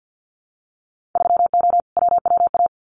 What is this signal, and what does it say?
Morse code, heard as a keyed tone, about 50 wpm, 700 Hz. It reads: 3O WWA
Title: Winter Field Day Exchanges Encore 50wpm